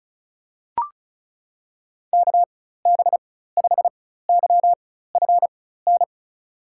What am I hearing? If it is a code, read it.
Morse code, heard as a keyed tone, about 36 wpm, spaced as for 16 wpm, 700 Hz. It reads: KB5YFN
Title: USA Call Signs - 4x Character Spacing - 35wpm